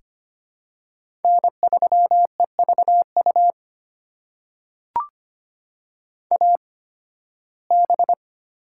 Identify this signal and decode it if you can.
Morse code, heard as a keyed tone, about 25 wpm, spaced as for 16 wpm, 700 Hz. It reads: N3EVU AB
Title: USA Call Signs - 5 Characters - 8x Character Spacing - 25wpm